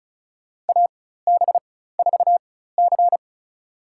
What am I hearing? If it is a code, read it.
Morse code, heard as a keyed tone, about 36 wpm, spaced as for 16 wpm, 700 Hz. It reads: AB4C
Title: USA Call Signs - 4 Characters - 4x Character Spacing - 35wpm